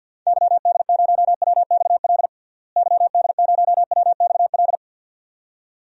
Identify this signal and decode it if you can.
Morse code, heard as a keyed tone, about 47 wpm, 700 Hz. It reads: YD0WXL YD0WXL
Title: DXCC-100 Call Signs - 6 Characters - Repeated 2x - 50wpm